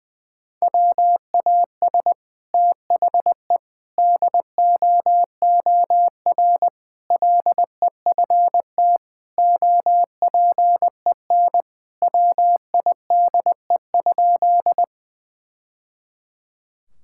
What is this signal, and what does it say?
Morse code, heard as a keyed tone, about 21 wpm, 700 Hz. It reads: WAS THE DOOR LEFT OPEN WIDE?
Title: Sentences from Top 1000 Words - Review Part 6 - 20wpm